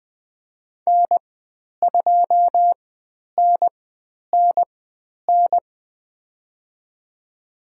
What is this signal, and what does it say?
Morse code, heard as a keyed tone, about 20 wpm, spaced as for 10 wpm, 700 Hz. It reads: N2NNN
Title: 102 Lesson 17g - Letter u - Call Signs 20wpm